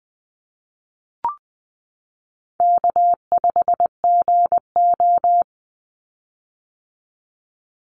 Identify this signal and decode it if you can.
Morse code, heard as a keyed tone, about 20 wpm, 700 Hz. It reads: K5GO
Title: Easy Call Signs - 20wpm